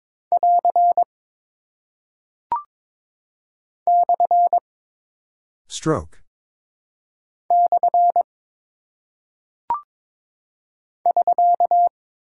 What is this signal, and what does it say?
Morse code, heard as a keyed tone, about 22 wpm, 700 Hz. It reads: <AR> / / <SK>
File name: practice / 22wpm / Single Punctuation-Prosign - 22wpm